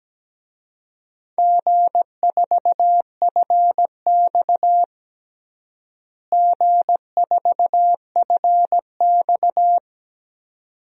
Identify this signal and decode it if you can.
Morse code, heard as a keyed tone, about 17 wpm, 700 Hz. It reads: G4FX G4FX
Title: DXCC-100 Call Signs - 4 Characters - Repeated 2x - 17wpm